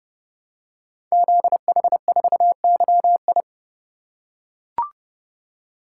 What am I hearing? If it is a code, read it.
Morse code, heard as a keyed tone, about 30 wpm, 700 Hz. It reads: ZH4YI
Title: Code Groups - 5 Letter:Numbers - 30wpm